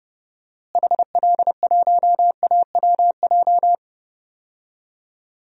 Morse code, heard as a keyed tone, about 31 wpm, 700 Hz. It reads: HL1AWJ
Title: DXCC-100 Call Signs - 6 Characters - Repeated 2x - 30wpm